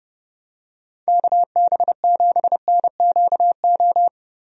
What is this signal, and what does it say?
Morse code, heard as a keyed tone, about 32 wpm, 700 Hz. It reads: KB7NQO
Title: USA Call Signs - 6 Characters - Repeated 3x - 30wpm